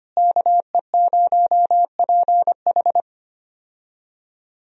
Morse code, heard as a keyed tone, about 25 wpm, 700 Hz. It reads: KE0PH
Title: USA Call Signs - 5 Characters - Repeated 3x - 25wpm